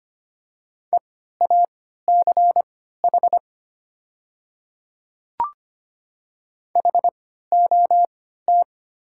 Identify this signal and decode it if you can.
Morse code, heard as a keyed tone, about 25 wpm, spaced as for 14 wpm, 700 Hz. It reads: EACH HOT
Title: 3x Character Spacing - Top 100 Words - 25wpm